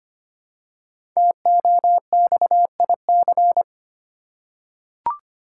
Morse code, heard as a keyed tone, about 25 wpm, 700 Hz. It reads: TOXIC
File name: 5 Letter Words - First 3 Letters Given - 25wpm